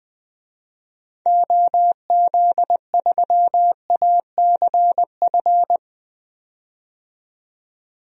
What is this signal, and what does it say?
Morse code, heard as a keyed tone, about 20 wpm, 700 Hz. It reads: OZ3ACF
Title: DXCC-100 Call Signs - 6 Characters - 20wpm